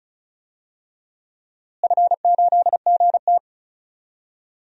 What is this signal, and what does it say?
Morse code, heard as a keyed tone, about 34 wpm, 700 Hz. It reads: F8GT
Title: DXCC-100 Call Signs - 4 Characters - Repeated 3x - 35wpm